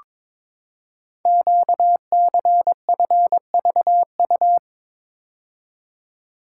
Morse code, heard as a keyed tone, about 22 wpm, 700 Hz. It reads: QCFVU
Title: Code Groups - 5 Letters - 22wpm